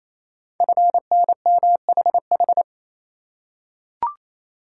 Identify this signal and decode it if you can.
Morse code, heard as a keyed tone, about 27 wpm, 700 Hz. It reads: FNMHH
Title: Code Groups - 5 Letters - 28wpm